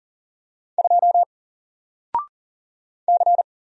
Morse code, heard as a keyed tone, about 40 wpm, 700 Hz. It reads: 2 C
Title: Single Character (Letter:Number:Punctuation:Prosign) Alt 40wpm